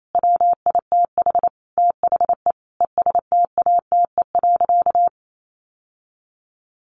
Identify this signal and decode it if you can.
Morse code, heard as a keyed tone, about 28 wpm, 700 Hz. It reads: WITH THE ESTATE.
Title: Sentences from Top 1200 Words 28wpm